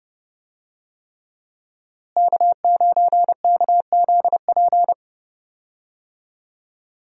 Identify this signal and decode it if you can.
Morse code, heard as a keyed tone, about 32 wpm, 700 Hz. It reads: K9KZP
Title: USA Call Signs - 5 Characters - Repeated 3x - 30wpm